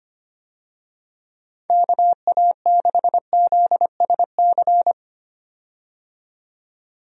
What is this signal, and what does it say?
Morse code, heard as a keyed tone, about 25 wpm, 700 Hz. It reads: KA6ZSC
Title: USA Call Signs - 6 Characters - 25wpm